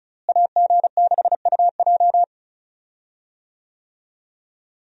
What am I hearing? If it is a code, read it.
Morse code, heard as a keyed tone, about 34 wpm, 700 Hz. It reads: AG6UJ
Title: USA Call Signs - 5 Characters - Repeated 2x - 35wpm